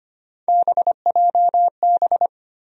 Morse code, heard as a keyed tone, about 25 wpm, 700 Hz. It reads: BJB
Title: USA Call Sign Postfixes - 3 Letters 25wpm